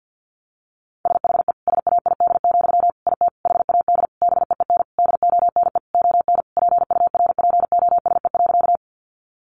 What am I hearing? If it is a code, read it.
Morse code, heard as a keyed tone, about 50 wpm, 700 Hz. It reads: SHE SAID, IT HAD BEEN DONE ON PURPOSE.
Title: Sentences from Top 800 Words 50wpm